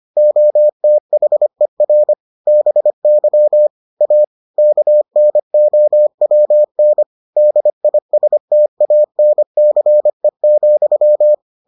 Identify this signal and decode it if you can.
Morse code, heard as a keyed tone, about 25 wpm, 600 Hz. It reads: OTHER BY A KNOWN DISTANCE,